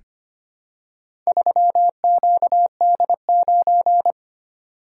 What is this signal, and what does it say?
Morse code, heard as a keyed tone, about 25 wpm, 700 Hz. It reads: 3QD9
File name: practice / 25wpm / Four Letter-Number Combinations - 25wpm